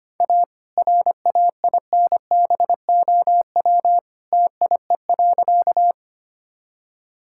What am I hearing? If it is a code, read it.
Morse code, heard as a keyed tone, about 25 wpm, 700 Hz. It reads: A RAINBOW TIE.
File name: practice / 25wpm / All-Age Appropriate Jokes - 25wpm